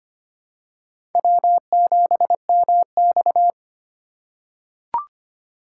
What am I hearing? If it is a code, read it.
Morse code, heard as a keyed tone, about 25 wpm, 700 Hz. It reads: W7MX
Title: USA Call Signs - 4 Characters - 2x Character Spacing - 25wpm